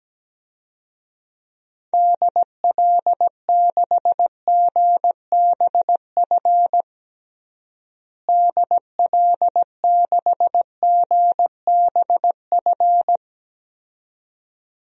Morse code, heard as a keyed tone, about 17 wpm, 700 Hz. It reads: DL6GBF DL6GBF
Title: DXCC-100 Call Signs - 6 Characters - Repeated 2x - 17wpm